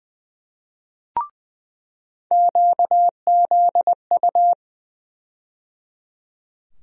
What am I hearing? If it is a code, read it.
Morse code, heard as a keyed tone, about 20 wpm, 700 Hz. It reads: QZU
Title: USA Call Sign Postfixes - 3 Letters 20wpm